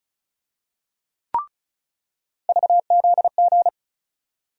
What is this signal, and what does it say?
Morse code, heard as a keyed tone, about 36 wpm, 700 Hz. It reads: VZG
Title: USA Call Sign Postfixes - 3 Letters 35wpm